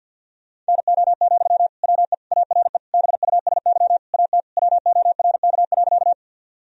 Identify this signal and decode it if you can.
Morse code, heard as a keyed tone, about 50 wpm, 700 Hz. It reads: NO, WE ARE BUSY AT WORK.
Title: Sentences from Top 1000 Words - Review Part 3 - 50wpm